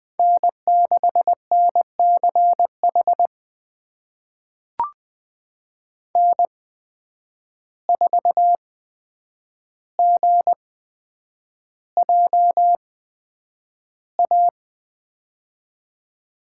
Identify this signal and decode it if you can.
Morse code, heard as a keyed tone, about 20 wpm, spaced as for 9 wpm, 700 Hz. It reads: N6NCH N4GJA
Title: USA Call Signs - 5 Characters - 8x Character Spacing - 20wpm